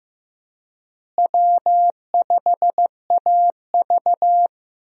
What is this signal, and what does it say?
Morse code, heard as a keyed tone, about 15 wpm, 700 Hz. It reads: W5AV
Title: USA Call Signs - 4 Characters - Repeated 3x - 15wpm